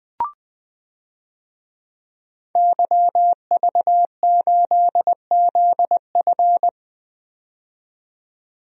Morse code, heard as a keyed tone, about 21 wpm, 700 Hz. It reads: YV8ZF
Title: DXCC-100 Call Signs - 5 Characters - 20wpm